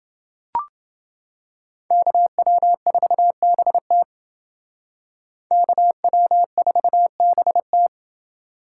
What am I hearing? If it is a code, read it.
Morse code, heard as a keyed tone, about 29 wpm, 700 Hz. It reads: KW4BT KW4BT